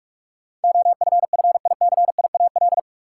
Morse code, heard as a keyed tone, about 46 wpm, 700 Hz. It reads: OFFICIAL
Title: Sentences from Top 1000 Words - New Word Drill - 45wpm